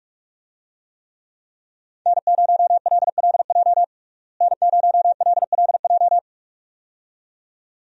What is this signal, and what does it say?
Morse code, heard as a keyed tone, about 46 wpm, 700 Hz. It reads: N0LLJ N0LLJ
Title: USA Call Signs - 5 Characters - Repeated 2x - 45wpm